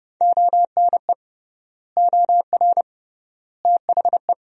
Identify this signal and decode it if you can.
Morse code, heard as a keyed tone, about 31 wpm, 700 Hz. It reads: ONE OR THE
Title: 3x Word-Spacing - Sentences from Top 400 Words - 30wpm